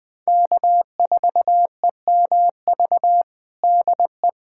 Morse code, heard as a keyed tone, about 20 wpm, 700 Hz. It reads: K4EMV DE
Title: USA Callsign DE Callsign - 5 Characters - 20wpm